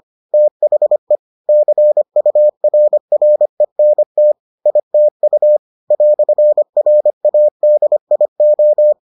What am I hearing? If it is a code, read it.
Morse code, heard as a keyed tone, about 25 wpm, 600 Hz. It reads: THE CURRENT ITU "RADIO